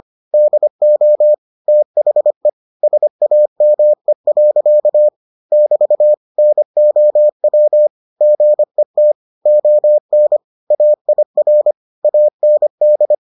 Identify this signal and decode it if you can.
Morse code, heard as a keyed tone, about 25 wpm, 600 Hz. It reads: DO THE SAME. <BT> NOW GET ON AIR AND